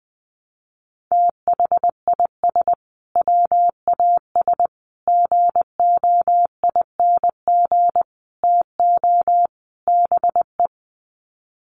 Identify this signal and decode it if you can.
Morse code, heard as a keyed tone, about 20 wpm, 700 Hz. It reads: THIS WAS GOING TO BE
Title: Sets of 5 Words Encore2 20wpm